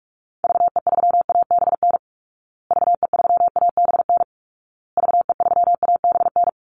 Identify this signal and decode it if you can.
Morse code, heard as a keyed tone, about 44 wpm, 700 Hz. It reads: VE3ABN VE3ABN VE3ABN
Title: Call Signs Repeated 2x - 45wpm